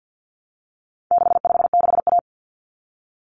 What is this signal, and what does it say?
Morse code, heard as a keyed tone, about 50 wpm, 700 Hz. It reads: 656A
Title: Common Cut Numbers - 4-Digit 50wpm